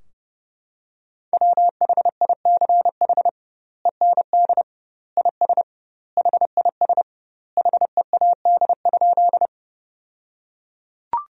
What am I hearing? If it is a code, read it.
Morse code, heard as a keyed tone, about 32 wpm, 700 Hz. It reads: WHICH END IS HIS HEAD?
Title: 2x Word-Spacing - Sentences from Top 300 Words - 30wpm